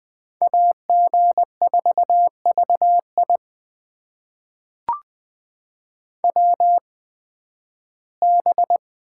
Morse code, heard as a keyed tone, about 20 wpm, spaced as for 13 wpm, 700 Hz. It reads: AG4VI WB
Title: USA Call Signs - 8x Character Spacing - 20wpm